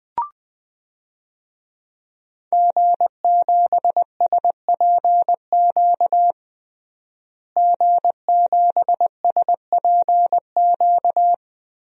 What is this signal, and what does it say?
Morse code, heard as a keyed tone, about 21 wpm, 700 Hz. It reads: G7SPQ G7SPQ
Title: DXCC-100 Call Signs - 5 Characters - Repeated 2x - 20wpm